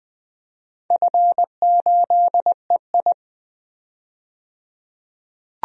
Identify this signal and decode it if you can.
Morse code, heard as a keyed tone, about 20 wpm, 700 Hz. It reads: F8EI
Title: DXCC-100 Call Signs - 4 Characters - 20wpm